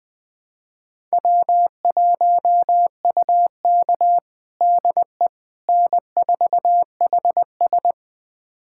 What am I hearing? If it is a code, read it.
Morse code, heard as a keyed tone, about 20 wpm, 700 Hz. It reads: W1UK DE N4HS
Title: USA Callsign DE Callsign - 4 Characters - 20wpm